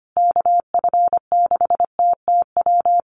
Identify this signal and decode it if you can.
Morse code, heard as a keyed tone, about 25 wpm, 700 Hz. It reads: KF6TTW
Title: USA Call Signs Repeated 3x 25wpm